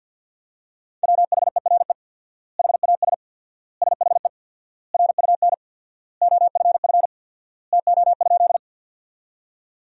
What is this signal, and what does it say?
Morse code, heard as a keyed tone, about 52 wpm, 700 Hz. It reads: WHERE HAS SHE RUN OFF TO?
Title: 4x Word-Spacing - Sentences from Top 300 Words - 50wpm